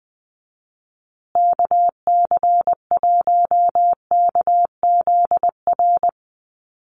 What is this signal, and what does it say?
Morse code, heard as a keyed tone, about 19 wpm, 700 Hz. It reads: KC1KZR
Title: USA Call Signs Repeated 2x 20wpm